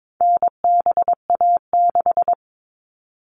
Morse code, heard as a keyed tone, about 22 wpm, 700 Hz. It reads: NBA6
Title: All Cut Numbers - 4-Digit 22wpm